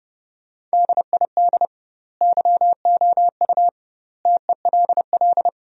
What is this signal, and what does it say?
Morse code, heard as a keyed tone, about 32 wpm, 700 Hz. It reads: DID YOU TELL
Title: 2x Word-Spacing - Sentences from Top 400 Words - 30wpm